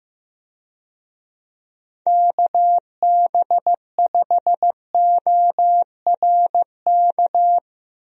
Morse code, heard as a keyed tone, about 15 wpm, 700 Hz. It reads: KB5ORK
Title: USA Call Signs - 6 Characters - Repeated 2x - 15wpm